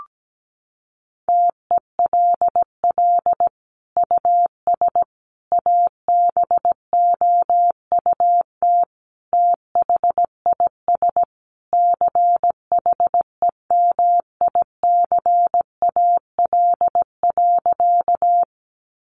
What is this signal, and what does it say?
Morse code, heard as a keyed tone, about 17 wpm, 700 Hz. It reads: TELL US ABOUT THIS CHEMICAL.
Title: Sentences from Top 1200 Words 17wpm